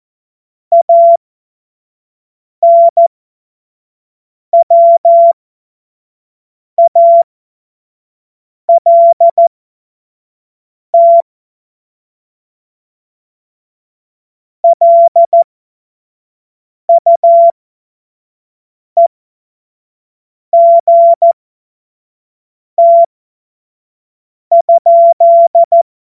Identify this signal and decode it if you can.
Morse code, heard as a keyed tone, about 14 wpm, spaced as for 5 wpm, 680 Hz. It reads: ANWALT LUEGT?